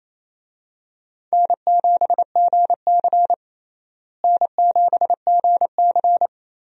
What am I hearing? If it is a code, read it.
Morse code, heard as a keyed tone, about 28 wpm, 700 Hz. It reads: N7GC N7GC
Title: USA Call Signs - 4 Characters - Repeated 2x - 28wpm